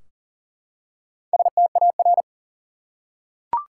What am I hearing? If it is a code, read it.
Morse code, heard as a keyed tone, about 40 wpm, 700 Hz. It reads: STAR
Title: Four Letter Words - 40wpm